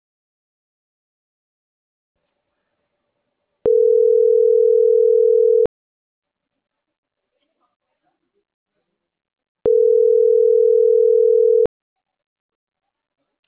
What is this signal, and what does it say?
Ringback tone (North America), 440 + 480 Hz.